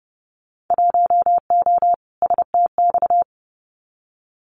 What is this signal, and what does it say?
Morse code, heard as a keyed tone, about 31 wpm, 700 Hz. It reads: 1O STX
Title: Winter Field Day Exchanges 30wpm